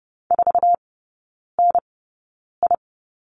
Morse code, heard as a keyed tone, about 30 wpm, spaced as for 9 wpm, 700 Hz. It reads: 4NI
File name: 081 Lesson 14e - Number 2 - Three Characters 30wpm